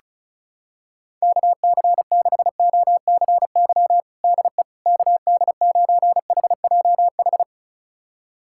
Morse code, heard as a keyed tone, about 34 wpm, 700 Hz. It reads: KC6OCY DE KD9HJH
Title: USA Callsign DE Callsign - 6 Characters - 35wpm